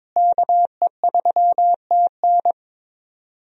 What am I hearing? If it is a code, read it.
Morse code, heard as a keyed tone, about 22 wpm, 700 Hz. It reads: KE3TN
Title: USA Call Signs - 5 Characters - Repeated 3x - 22wpm